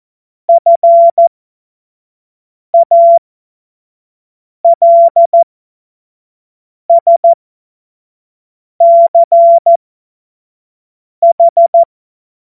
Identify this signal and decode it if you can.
Morse code, heard as a keyed tone, about 14 wpm, spaced as for 5 wpm, 680 Hz. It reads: FALSCH
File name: Morsekurs / Witze / Witz 08. Februar 2022 25 F70